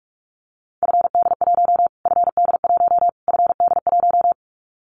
Morse code, heard as a keyed tone, about 44 wpm, 700 Hz. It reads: FD1 FD1 FD1
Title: Callsign Prefixes for Top 25 DXCC - Repeated 3x - 45wpm